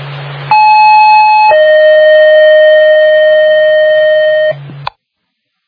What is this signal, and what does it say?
Two-tone page, heard as a one-second tone, then a three-second tone, about 860 then 620 Hz, with noise. It is heard from 0.5 to 4.5 s.